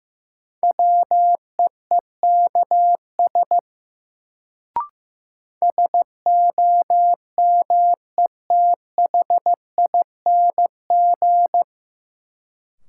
Morse code, heard as a keyed tone, about 15 wpm, 700 Hz. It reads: WEEKS SOMETHING
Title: Sets of 5 Words - New Word Drill - 15wpm